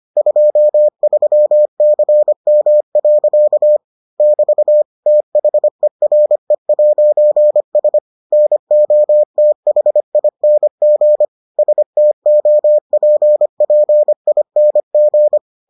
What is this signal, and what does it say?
Morse code, heard as a keyed tone, about 25 wpm, 600 Hz. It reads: 23CM. <BT> THERE'S NOTHING STOPPING